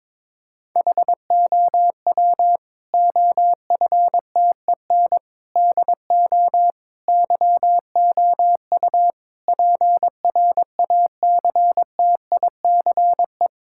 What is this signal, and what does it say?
Morse code, heard as a keyed tone, about 22 wpm, 700 Hz. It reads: HOW OFTEN DO YOU PRACTICE
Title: Sentences from Top 1000 Words - Review Part 6 - 22wpm